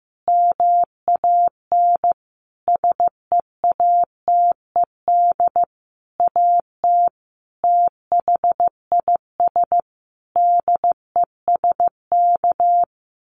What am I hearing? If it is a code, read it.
Morse code, heard as a keyed tone, about 15 wpm, 700 Hz. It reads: MAN SEATED AT THIS DESK
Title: A Sound Of Thunder 15wpm